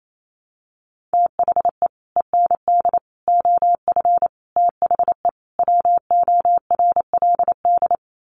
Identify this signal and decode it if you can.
Morse code, heard as a keyed tone, about 28 wpm, 700 Hz. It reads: THE END OF THE WORLD
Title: Sets of 5 Words 28wpm